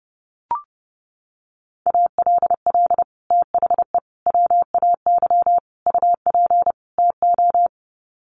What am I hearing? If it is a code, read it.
Morse code, heard as a keyed tone, about 30 wpm, 700 Hz. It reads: ALL THE WAY UP TO
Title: Sets of 5 Words Encore 30wpm